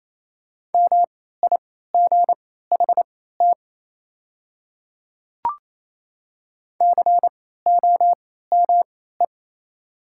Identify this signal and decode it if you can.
Morse code, heard as a keyed tone, about 28 wpm, spaced as for 16 wpm, 690 Hz. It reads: MIGHT COME
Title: 3x Character Spacing - Top 500 Words - 28wpm